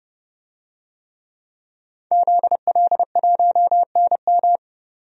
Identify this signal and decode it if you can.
Morse code, heard as a keyed tone, about 30 wpm, 700 Hz. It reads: ZL1NM
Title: DXCC-100 Call Signs - 5 Characters - 30wpm